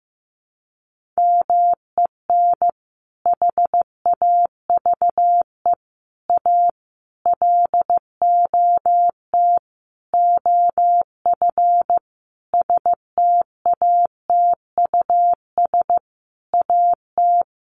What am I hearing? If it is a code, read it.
Morse code, heard as a keyed tone, about 15 wpm, 700 Hz. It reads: MEN HAVE A LOT OF STATUS AT
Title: Sentences from Top 1400 Words 15wpm